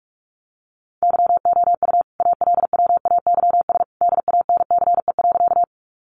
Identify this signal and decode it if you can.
Morse code, heard as a keyed tone, about 44 wpm, 700 Hz. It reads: YOU ALWAYS DANCE.
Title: Sentences from Top 700 Words 45wpm